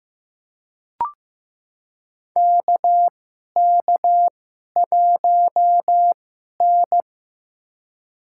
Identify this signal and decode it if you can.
Morse code, heard as a keyed tone, about 15 wpm, spaced as for 11 wpm, 700 Hz. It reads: KK1N
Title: USA Call Signs - 4 Characters - 2x Character Spacing - 15wpm